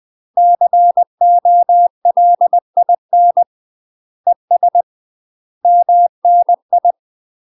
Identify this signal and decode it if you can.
Morse code, heard as a keyed tone, about 20 wpm, 700 Hz. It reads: COLIN ES MNI
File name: UK MASTER QSOs TRAINING / 20WPM / UK MASTER QSOs 0040